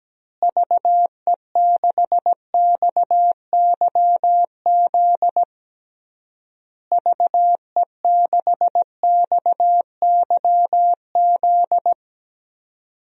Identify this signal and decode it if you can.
Morse code, heard as a keyed tone, about 17 wpm, 700 Hz. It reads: VE6XYZ VE6XYZ
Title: DXCC-100 Call Signs - 6 Characters - Repeated 3x - 17wpm